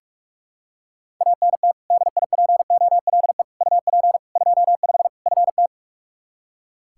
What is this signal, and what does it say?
Morse code, heard as a keyed tone, about 46 wpm, 700 Hz. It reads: ANT DIPOLE UP 25 FT
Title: Practice QSO - Exchange 2 - Info - 45wpm